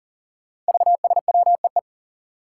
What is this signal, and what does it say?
Morse code, heard as a keyed tone, about 40 wpm, 700 Hz. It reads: VSWEE
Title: Five Challenging Letter-Number Combinations 40wpm